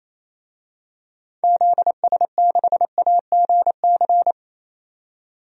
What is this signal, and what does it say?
Morse code, heard as a keyed tone, about 28 wpm, 700 Hz. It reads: ZS6AGC
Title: DXCC-100 Call Signs - 6 Characters - 28wpm